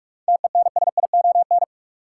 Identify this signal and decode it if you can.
Morse code, heard as a keyed tone, about 42 wpm, 700 Hz. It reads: TENSION
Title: Sentences from Top 1600 Words - New Word Drill Alt - 45wpm